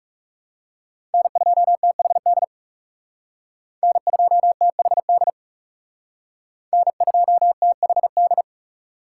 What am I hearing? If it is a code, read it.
Morse code, heard as a keyed tone, about 38 wpm, 700 Hz. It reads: N2THD N2THD N2THD